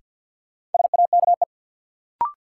Morse code, heard as a keyed tone, about 46 wpm, 700 Hz. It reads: SAKE
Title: Sets of 4 Words - New Word Drill - 50wpm